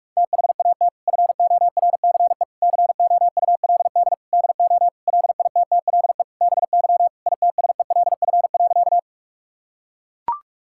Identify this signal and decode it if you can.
Morse code, heard as a keyed tone, about 43 wpm, 700 Hz. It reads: THAT FORCE COULD DO LITTLE BY ITSELF.
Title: Sentences from Top 1000 Words - Review Part 5 - 45wpm